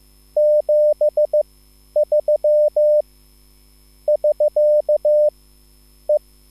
Morse code, heard as a keyed tone, about 15 wpm, spaced as for 10 wpm, 600 Hz. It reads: 73 <SK> E
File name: QSL / CW lesson